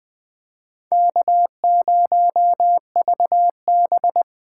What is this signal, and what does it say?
Morse code, heard as a keyed tone, about 20 wpm, 700 Hz. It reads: K0VB
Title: USA Call Signs - 4 Characters - 2x Character Spacing - 20wpm